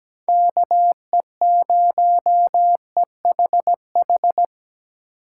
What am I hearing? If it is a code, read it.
Morse code, heard as a keyed tone, about 20 wpm, 700 Hz. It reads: KE0EHH
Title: Speed Racing - USA 2x3 Call Signs 17wpm